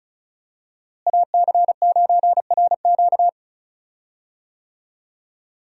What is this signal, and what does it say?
Morse code, heard as a keyed tone, about 36 wpm, 700 Hz. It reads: AC9RQ
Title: USA Call Signs - 5 Characters - Repeated 2x - 35wpm